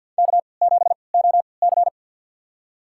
Morse code, heard as a keyed tone, about 53 wpm, spaced as for 29 wpm, 700 Hz. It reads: K7OC
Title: USA Call Signs - 4 Characters - 3x Character Spacing - 50wpm